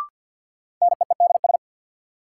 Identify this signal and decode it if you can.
Morse code, heard as a keyed tone, about 52 wpm, 700 Hz. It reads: NEEDS
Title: Sets of 3 Words - New Word Drill - 50wpm